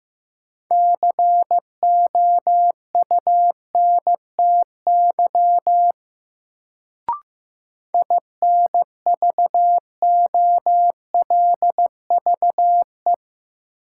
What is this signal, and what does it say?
Morse code, heard as a keyed tone, about 15 wpm, 700 Hz. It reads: COUNTY INVOLVE
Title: Sentences from Top 1200 Words - New Word Drill - 15wpm